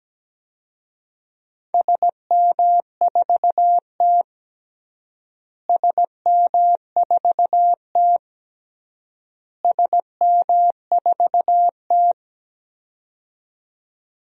Morse code, heard as a keyed tone, about 17 wpm, 700 Hz. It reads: SM4T SM4T SM4T
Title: DXCC-100 Call Signs - 4 Characters - Repeated 3x - 17wpm